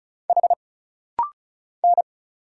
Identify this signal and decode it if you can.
Morse code, heard as a keyed tone, about 36 wpm, 700 Hz. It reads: H N
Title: Single Letters - Rapid-Fire 35wpm